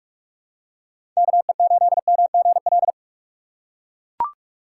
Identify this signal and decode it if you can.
Morse code, heard as a keyed tone, about 41 wpm, 700 Hz. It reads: KE8MGL
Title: USA Call Signs - 6 Characters - 4x Character Spacing - 45wpm